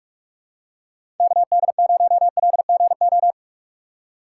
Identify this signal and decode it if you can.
Morse code, heard as a keyed tone, about 41 wpm, 700 Hz. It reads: KD0LGO
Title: USA Call Signs - 4x Character Spacing - 45wpm